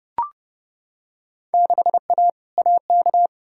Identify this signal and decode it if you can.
Morse code, heard as a keyed tone, about 32 wpm, 700 Hz. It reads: 6A AK
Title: Field Day Exchanges Encore - 30wpm